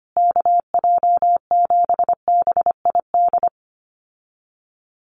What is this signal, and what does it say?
Morse code, heard as a keyed tone, about 25 wpm, 700 Hz. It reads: KJ7BID
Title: USA Call Signs Repeated 3x 25wpm